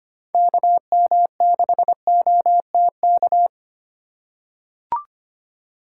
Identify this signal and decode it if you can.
Morse code, heard as a keyed tone, about 25 wpm, 690 Hz. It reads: KM6OTK
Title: USA Call Signs - 6 Characters - 3x Character Spacing - 25wpm